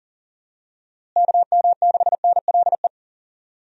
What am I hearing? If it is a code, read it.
Morse code, heard as a keyed tone, about 40 wpm, 700 Hz. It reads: KM6NLE
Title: USA Call Signs - 6 Characters - Repeated 3x - 40wpm